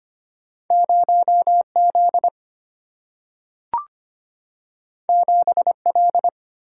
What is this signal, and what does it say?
Morse code, heard as a keyed tone, about 25 wpm, 700 Hz. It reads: 0 Z 7 L
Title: Single-Letter Single-Number 25wpm